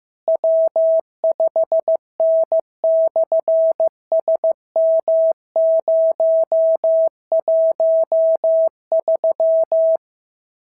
Morse code, heard as a keyed tone, about 15 wpm, 650 Hz. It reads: W5N/SM013
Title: SOTA QSO Practice 15wpm